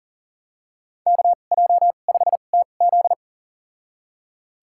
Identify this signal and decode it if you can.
Morse code, heard as a keyed tone, about 40 wpm, spaced as for 29 wpm, 700 Hz. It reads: KJ5TZ
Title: USA Call Signs - 5 Characters - 2x Character Spacing - 40wpm